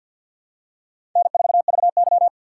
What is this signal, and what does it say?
Morse code, heard as a keyed tone, about 53 wpm, 700 Hz. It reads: N4VY
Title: USA Call Signs - 4 Characters - 2x Character Spacing - 50wpm